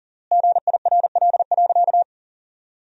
Morse code, heard as a keyed tone, about 37 wpm, 700 Hz. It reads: GIRL.
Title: 3x Word-Spacing - Sentences from Top 400 Words - 40wpm